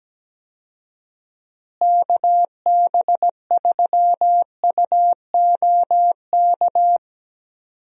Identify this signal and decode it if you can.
Morse code, heard as a keyed tone, about 17 wpm, 700 Hz. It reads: KB3UOK
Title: USA Call Signs - 6 Characters - Repeated 3x - 17wpm